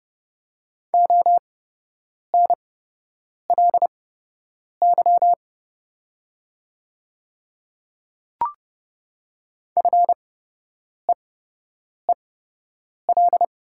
Morse code, heard as a keyed tone, about 32 wpm, spaced as for 8 wpm, 700 Hz. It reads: ONLY FEEL
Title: 8x Character Spacing - Top 500 Words - 30wpm